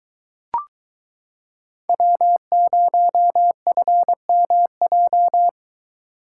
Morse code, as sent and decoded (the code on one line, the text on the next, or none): .-- ----- ..-. -- .---
W0FMJ